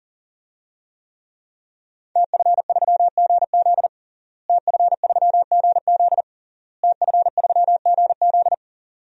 Morse code, heard as a keyed tone, about 40 wpm, 700 Hz. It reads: TF3GZ TF3GZ TF3GZ
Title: DXCC-100 Call Signs - 5 Characters - Repeated 3x - 40wpm